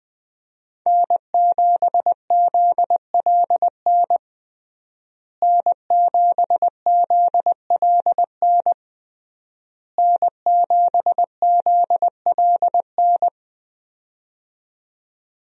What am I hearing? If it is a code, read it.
Morse code, heard as a keyed tone, about 20 wpm, 700 Hz. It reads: N7ZLN N7ZLN N7ZLN
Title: USA Call Signs - 5 Characters - Repeated 3x - 20wpm